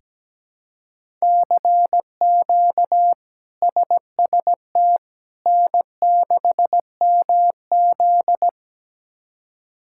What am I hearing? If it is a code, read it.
Morse code, heard as a keyed tone, about 17 wpm, 700 Hz. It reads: CQ SST N6MZ
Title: K1USN SST QSO Practice - 17wpm